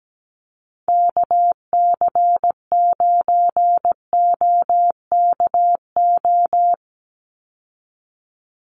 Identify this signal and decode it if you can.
Morse code, heard as a keyed tone, about 17 wpm, 700 Hz. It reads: KC9OKO
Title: USA Call Signs 17wpm